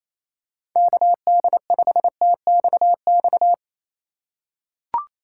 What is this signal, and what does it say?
Morse code, heard as a keyed tone, about 27 wpm, 700 Hz. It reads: KD5TXX
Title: K1USN SST QSO Practice - 28wpm